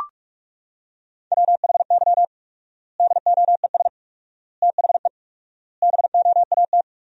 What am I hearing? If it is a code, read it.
Morse code, heard as a keyed tone, about 41 wpm, 700 Hz. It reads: WHY DOES THE BOAT
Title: 4x Word-Spacing - Sentences from Top 500 Words - 45wpm